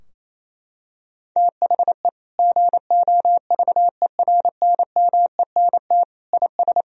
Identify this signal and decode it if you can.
Morse code, heard as a keyed tone, about 28 wpm, 700 Hz. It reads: THE GOVERNMENT IS
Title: Sentences from Top 1000 Words - Review Part 4 - 28wpm